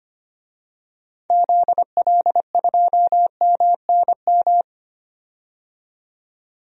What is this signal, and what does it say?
Morse code, heard as a keyed tone, about 25 wpm, 700 Hz. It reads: ZL2MNM
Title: DXCC-100 Call Signs - 6 Characters - 25wpm